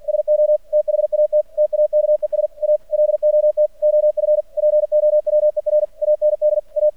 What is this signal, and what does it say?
Morse code, heard as a keyed tone, about 46 wpm, 600 Hz. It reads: SO THAT TAKES A LOT OF POWER AND A
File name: archived / CRSnet